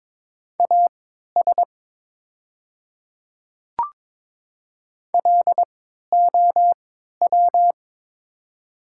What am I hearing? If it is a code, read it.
Morse code, heard as a keyed tone, about 22 wpm, spaced as for 12 wpm, 700 Hz. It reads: AS LOW
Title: 3x Character Spacing - Top 200 Words - 22wpm